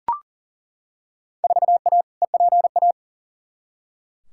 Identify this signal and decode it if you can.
Morse code, heard as a keyed tone, about 40 wpm, 700 Hz. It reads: 4A EPA
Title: Field Day Exchanges - 40wpm